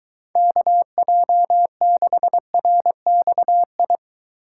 Morse code, heard as a keyed tone, about 23 wpm, 700 Hz. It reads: KJ6RXI